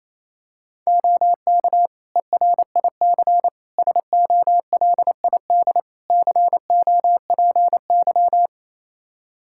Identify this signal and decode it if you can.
Morse code, heard as a keyed tone, about 28 wpm, 700 Hz. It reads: OK ERIC SOLID COPY
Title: Practice QSO - Exchange 3 - More Info - 28wpm